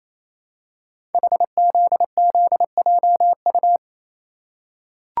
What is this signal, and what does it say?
Morse code, heard as a keyed tone, about 28 wpm, 700 Hz. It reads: HZZJU
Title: Code Groups - 5 Letters - 28wpm